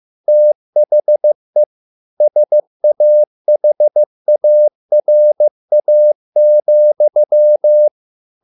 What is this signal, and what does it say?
Morse code, heard as a keyed tone, about 15 wpm, 600 Hz. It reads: THE SAHARA,